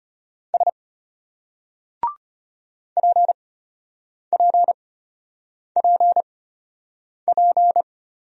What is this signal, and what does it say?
Morse code, heard as a keyed tone, about 33 wpm, spaced as for 8 wpm, 700 Hz. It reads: S PPPP